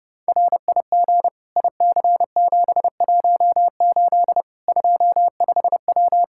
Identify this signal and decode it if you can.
Morse code, heard as a keyed tone, about 30 wpm, 700 Hz. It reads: RIG IC718 25W
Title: K1USN SST QSO Practice 30wpm